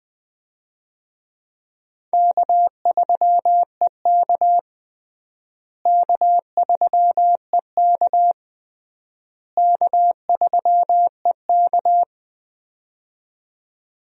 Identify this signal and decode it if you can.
Morse code, heard as a keyed tone, about 21 wpm, 700 Hz. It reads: K3EK K3EK K3EK
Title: USA Call Signs - 4 Characters - Repeated 3x - 20wpm